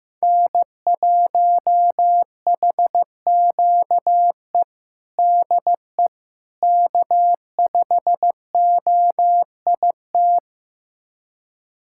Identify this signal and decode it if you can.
Morse code, heard as a keyed tone, about 15 wpm, 700 Hz. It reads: N1HQE DE K5OIT
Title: USA Callsign DE Callsign - 5 Characters - 15wpm